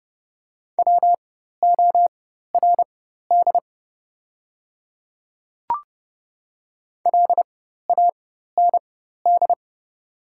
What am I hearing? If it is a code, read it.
Morse code, heard as a keyed tone, about 30 wpm, spaced as for 14 wpm, 700 Hz. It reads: WORD LAND
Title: 4x Character Spacing - Top 200 Words - 30wpm